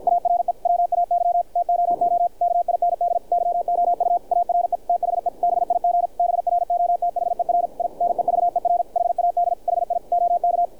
Morse code, heard as a keyed tone, about 50 wpm, 700 Hz, with radio noise. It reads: ARE WAY TOO KIND YOU ARE THE BIG BROTHER I NEVER HAD HI OK
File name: archived / CRSnet